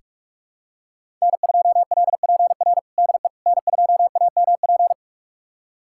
Morse code, heard as a keyed tone, about 42 wpm, 700 Hz. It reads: N2LPR DE N2AMP
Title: USA Callsign DE Callsign - 5 Characters - 45wpm